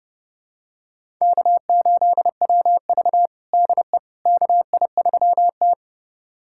Morse code, heard as a keyed tone, about 30 wpm, 700 Hz. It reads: K8WV DE KI3T
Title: USA Callsign DE Callsign - 4 Characters - 30wpm